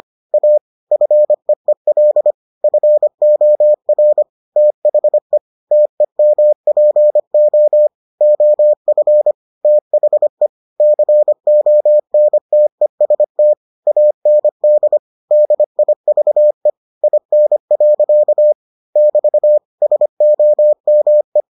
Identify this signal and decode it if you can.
Morse code, heard as a keyed tone, about 25 wpm, 600 Hz. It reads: A FEEL FOR THE TEMPO OF THE CONTEST AND DIVE IN. <BT> SOME